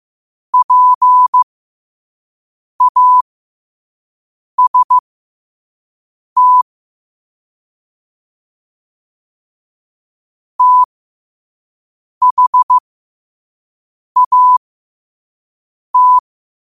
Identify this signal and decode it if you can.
Morse code, heard as a keyed tone, about 15 wpm, spaced as for 5 wpm, 1000 Hz. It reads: PAST THAT
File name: qotdfiles / 5wpm